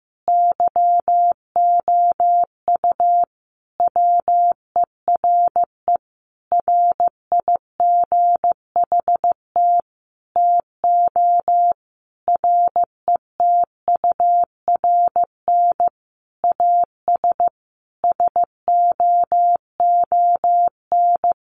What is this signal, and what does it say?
Morse code, heard as a keyed tone, about 15 wpm, 700 Hz. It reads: YOU WERE RIGHT TO RETURN AS SOON
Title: Sentences from Top 800 Words 15wpm